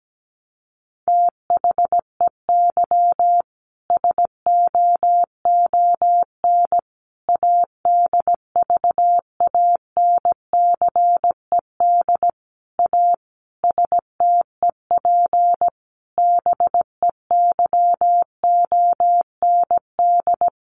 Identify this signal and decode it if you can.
Morse code, heard as a keyed tone, about 17 wpm, 700 Hz. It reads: THEY SOON ADVANCED A STEP BEYOND
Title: Sentences from Top 1200 Words 17wpm